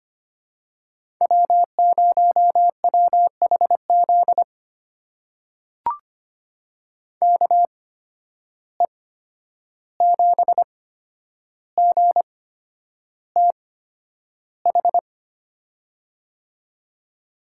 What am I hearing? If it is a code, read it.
Morse code, heard as a keyed tone, about 25 wpm, spaced as for 10 wpm, 700 Hz. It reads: W0WHZ KE7GTH
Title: USA Call Signs - 8x Character Spacing - 25wpm